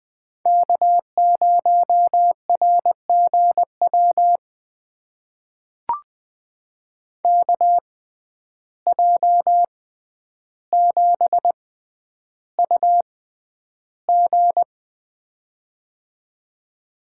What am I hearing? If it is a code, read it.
Morse code, heard as a keyed tone, about 21 wpm, spaced as for 10 wpm, 700 Hz. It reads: K0RGW KJ7UG
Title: USA Call Signs - 5 Characters - 6x Character Spacing - 20wpm